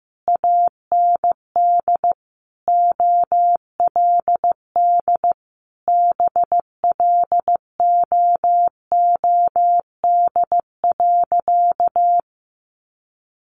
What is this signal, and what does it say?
Morse code, heard as a keyed tone, about 15 wpm, 700 Hz. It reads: AND OLD BLOOD.
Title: A Sound Of Thunder 15wpm